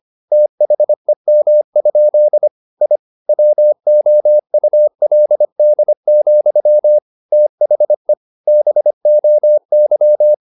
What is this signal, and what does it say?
Morse code, heard as a keyed tone, about 25 wpm, 600 Hz. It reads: THEM? I WOULD, THE BOY
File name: the old man and the sea part 1 25wpm